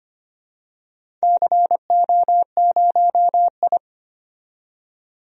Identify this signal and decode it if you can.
Morse code, heard as a keyed tone, about 25 wpm, 700 Hz. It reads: CO0I
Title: DXCC-100 Call Signs - 4 Characters - 25wpm